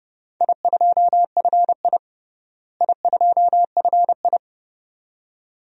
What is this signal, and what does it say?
Morse code, heard as a keyed tone, about 30 wpm, 700 Hz. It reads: I2FI I2FI
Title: DXCC-100 Call Signs - 4 Characters - Repeated 2x - 30wpm